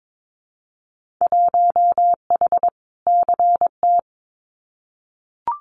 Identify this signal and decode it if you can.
Morse code, heard as a keyed tone, about 22 wpm, 700 Hz. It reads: 1H CT
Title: Winter Field Day Exchanges Encore 22wpm